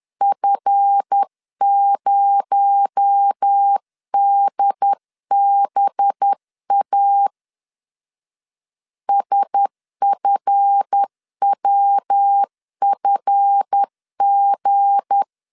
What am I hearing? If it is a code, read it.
Morse code, heard as a keyed tone, about 10 wpm, 800 Hz. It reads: F0DBA SFWFG
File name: LEZIONI / 7 GF9Z / 7 GF 10wpm 2514